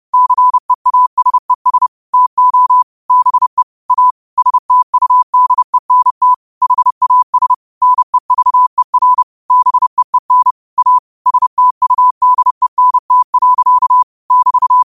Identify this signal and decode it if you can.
Morse code, heard as a keyed tone, about 30 wpm, 1000 Hz. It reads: CEASES TO BE A STUDENT HAS NEVER BEEN A STUDENT. <BT>